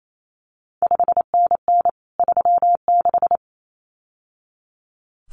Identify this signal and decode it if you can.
Morse code, heard as a keyed tone, about 28 wpm, 700 Hz. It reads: 5NN 36
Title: CQ WW DX Contest Exchange Encore - 28wpm